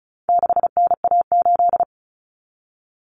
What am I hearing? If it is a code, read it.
Morse code, heard as a keyed tone, about 34 wpm, 700 Hz. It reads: 6NA8
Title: Common Cut Numbers - 4-Digit 35wpm